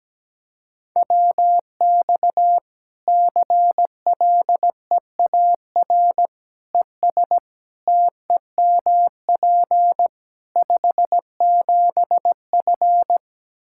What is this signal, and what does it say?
Morse code, heard as a keyed tone, about 17 wpm, 700 Hz. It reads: WX CLEAR ES TEMP 57F
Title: Practice QSO - Exchange 2 - Info - 17wpm